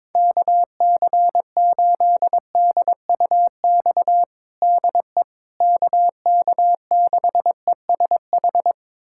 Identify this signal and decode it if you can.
Morse code, heard as a keyed tone, about 22 wpm, 700 Hz. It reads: KC8DUX DE KK6ESH
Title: USA Callsign DE Callsign 22wpm